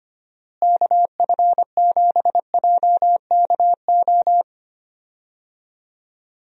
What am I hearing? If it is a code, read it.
Morse code, heard as a keyed tone, about 25 wpm, 700 Hz. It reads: KF7JKO
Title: USA Call Signs - 6 Characters - Repeated 2x - 25wpm